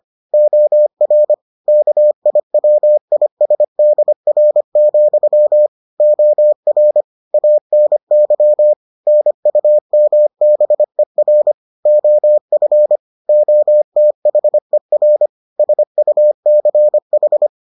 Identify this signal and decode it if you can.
Morse code, heard as a keyed tone, about 25 wpm, 600 Hz. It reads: OR KIWISDR, OR ANY NUMBER OF OTHER SUCH